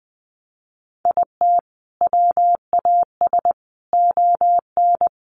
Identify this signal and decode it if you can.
Morse code, heard as a keyed tone, about 20 wpm, 700 Hz. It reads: IT WAS ON
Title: Sentences from Top 800 Words 20wpm